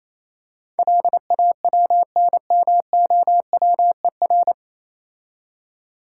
Morse code, heard as a keyed tone, about 28 wpm, 690 Hz. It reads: LAWNMOWER